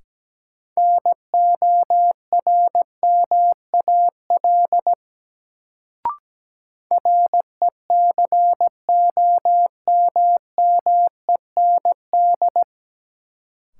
Morse code, heard as a keyed tone, about 17 wpm, 700 Hz. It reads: NORMAL RECOMMEND
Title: Sentences from Top 1200 Words - New Word Drill - 17wpm